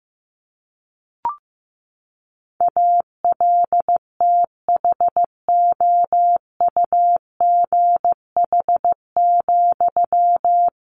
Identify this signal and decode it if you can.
Morse code, heard as a keyed tone, about 15 wpm, 700 Hz. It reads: ALTHOUGH,
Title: Sentences from Top 600 Words 15wpm